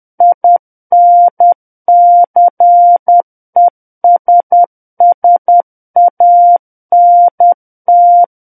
Morse code, heard as a keyed tone, about 10 wpm, 700 Hz. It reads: INCESSANT
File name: Book One / 01-Chapter 1 - 10 WPM